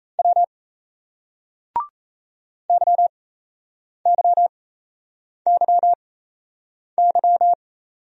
Morse code, heard as a keyed tone, about 38 wpm, spaced as for 8 wpm, 700 Hz. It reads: W YYYY